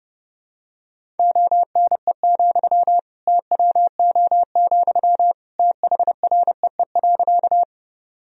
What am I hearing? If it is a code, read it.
Morse code, heard as a keyed tone, about 30 wpm, 700 Hz. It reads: ONE, TWO, THREE.
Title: Sentences from Top 1000 Words - Review Part 1 - 30wpm